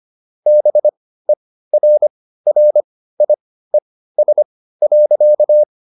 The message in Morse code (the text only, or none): BERRIES.